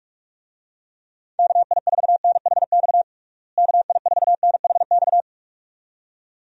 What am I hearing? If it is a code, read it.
Morse code, heard as a keyed tone, about 41 wpm, 700 Hz. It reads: KI4NHX KI4NHX
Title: USA Call Signs - 6 Characters - Repeated 2x - 45wpm